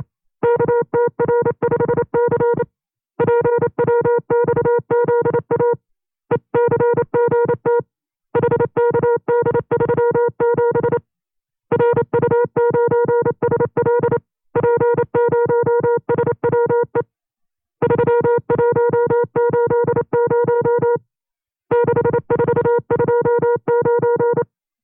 Morse code, heard as a keyed tone, about 28 wpm, 480 Hz. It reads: KTR5C PWXZA ECGT HKD37 RU9SL P0SWE 3180 6429
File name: 25WPM